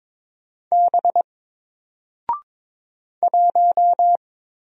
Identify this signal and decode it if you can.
Morse code, heard as a keyed tone, about 22 wpm, 700 Hz. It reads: B 1
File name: Single-Letter-Number-22wpm